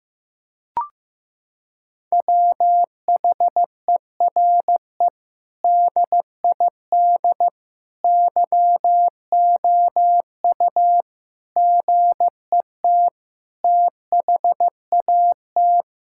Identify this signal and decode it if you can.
Morse code, heard as a keyed tone, about 15 wpm, 700 Hz. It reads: WHERE DID YOU GET THAT
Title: Sentences from Top 1600 Words 15wpm